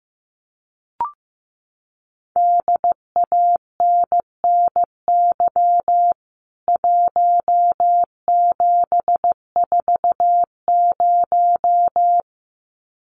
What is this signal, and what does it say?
Morse code, heard as a keyed tone, about 15 wpm, 700 Hz. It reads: DANNY 1740
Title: CWT Exchange 15wpm